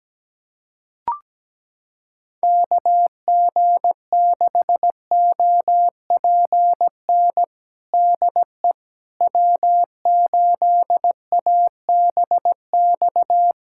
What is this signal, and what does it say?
Morse code, heard as a keyed tone, about 17 wpm, 700 Hz. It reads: KG6OPN DE W8ABX
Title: USA Callsign DE Callsign 17wpm